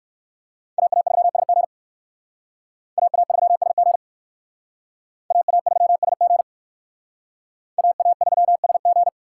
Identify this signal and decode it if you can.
Morse code, heard as a keyed tone, about 57 wpm, 700 Hz. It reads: AA3SG AA3SG AA3SG AA3SG